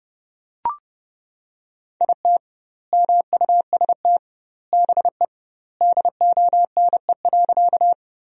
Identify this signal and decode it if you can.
Morse code, heard as a keyed tone, about 30 wpm, 700 Hz. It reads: IT MUST BE DONE.
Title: 2x Word-Spacing - Sentences from Top 500 Words - 30wpm